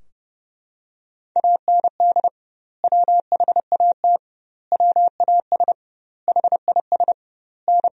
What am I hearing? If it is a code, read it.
Morse code, heard as a keyed tone, about 32 wpm, 700 Hz. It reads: AND WHAT WAS HIS N
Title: 2x Word-Spacing - Sentences from Top 200 Words - 30wpm